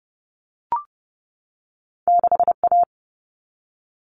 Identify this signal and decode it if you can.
Morse code, heard as a keyed tone, about 30 wpm, 700 Hz. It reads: 6A
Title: All Cut Numbers - 2-Digit 30wpm